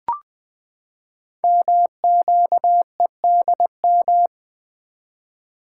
Morse code, heard as a keyed tone, about 20 wpm, 700 Hz. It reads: MQEDM
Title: Code Groups - 5 Letter:Numbers - 20wpm